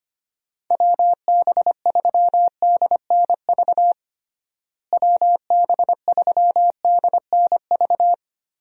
Morse code, heard as a keyed tone, about 25 wpm, 700 Hz. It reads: WB3DNV WB3DNV
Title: USA Call Signs - 6 Characters - Repeated 2x - 25wpm